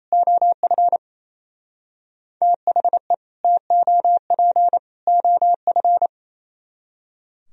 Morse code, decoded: OF THE TOP OF

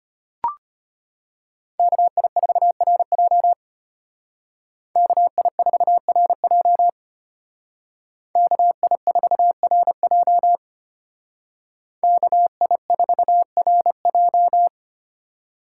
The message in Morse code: KI4RJ KI4RJ KI4RJ KI4RJ

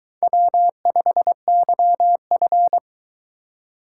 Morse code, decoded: W5YF

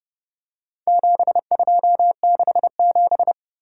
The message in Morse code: 7267

